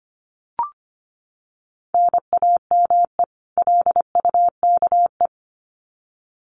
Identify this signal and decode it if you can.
Morse code, heard as a keyed tone, about 25 wpm, 700 Hz. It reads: NAME LUKE
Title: Common QSO Phrases 25wpm